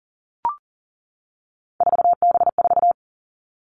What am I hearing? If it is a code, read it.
Morse code, heard as a keyed tone, about 39 wpm, 700 Hz. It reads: 4B4
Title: Three Challenging Letter-Number Combinations 40wpm